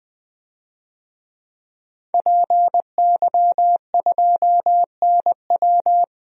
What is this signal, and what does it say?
Morse code, heard as a keyed tone, about 20 wpm, 700 Hz. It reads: PY2NW
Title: DXCC-100 Call Signs - 5 Characters - 20wpm